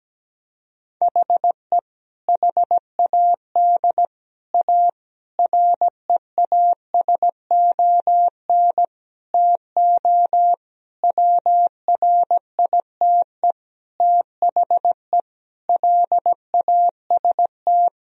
Morse code, heard as a keyed tone, about 17 wpm, 700 Hz. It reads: HE HAD A REASON TO WRITE THE LAST
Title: Sentences from Top 1000 Words - Review Part 1 - 17wpm